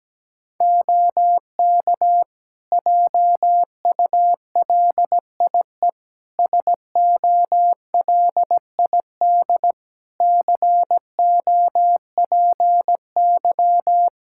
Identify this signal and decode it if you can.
Morse code, heard as a keyed tone, about 17 wpm, 700 Hz. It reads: OK JULIE SOLID COPY
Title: Practice QSO - Exchange 3 - More Info - 17wpm